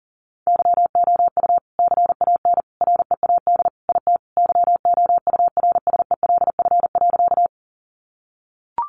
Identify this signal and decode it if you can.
Morse code, heard as a keyed tone, about 40 wpm, 700 Hz. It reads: YOU CAN READ IT YOURSELF.
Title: Sentences from Top 600 Words 40wpm